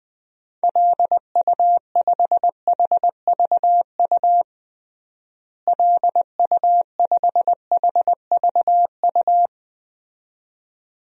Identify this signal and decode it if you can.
Morse code, heard as a keyed tone, about 20 wpm, 700 Hz. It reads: LU5HVU LU5HVU
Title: DXCC-100 Call Signs - 6 Characters - Repeated 2x - 20wpm